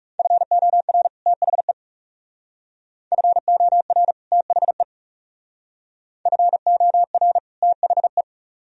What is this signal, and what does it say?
Morse code, heard as a keyed tone, about 36 wpm, 700 Hz. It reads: FOR THE FOR THE FOR THE
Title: Speed-Racing - Sets of 2 Words 30wpm